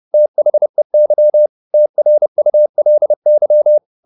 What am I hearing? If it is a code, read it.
Morse code, heard as a keyed tone, about 30 wpm, 600 Hz. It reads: THEY TRULY